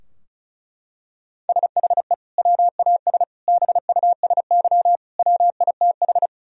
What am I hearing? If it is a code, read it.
Morse code, heard as a keyed tone, about 34 wpm, 700 Hz. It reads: SHE WAS BUSY WITH
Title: Sentences from Top 1000 Words - Review Part 2 - 35wpm